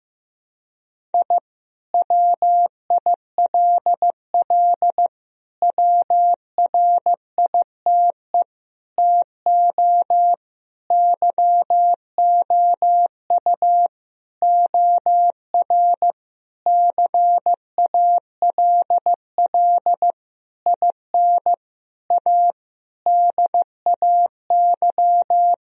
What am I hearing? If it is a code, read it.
Morse code, heard as a keyed tone, about 15 wpm, 700 Hz. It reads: I WILL WRITE TO YOU OR CALL IN A DAY
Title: Sentences from Top 1000 Words - Review Part 1 - 15wpm